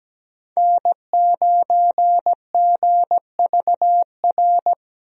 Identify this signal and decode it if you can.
Morse code, heard as a keyed tone, about 17 wpm, 700 Hz. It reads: N9GVR
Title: USA Call Signs - 5 Characters - Repeated 2x - 17wpm